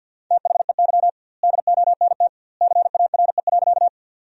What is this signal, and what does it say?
Morse code, heard as a keyed tone, about 50 wpm, 700 Hz. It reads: THEY DONT CARE.
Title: 2x Word-Spacing - Sentences from Top 300 Words - 50wpm